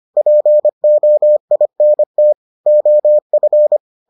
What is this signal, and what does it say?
Morse code, heard as a keyed tone, about 25 wpm, 600 Hz. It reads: POINT OF